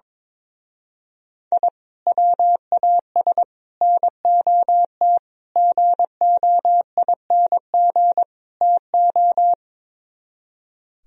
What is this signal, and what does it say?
Morse code, heard as a keyed tone, about 22 wpm, 700 Hz. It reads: I WAS NOT GOING TO
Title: Speed-Racing - Sets of 5 Words 22wpm